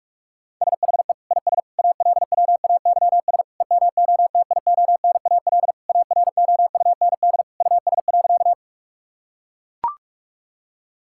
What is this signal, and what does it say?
Morse code, heard as a keyed tone, about 42 wpm, 700 Hz. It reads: SHE IS ALWAYS EMOTIONAL AROUND US.
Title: Sentences from Top 1000 Words - Review Part 5 - 45wpm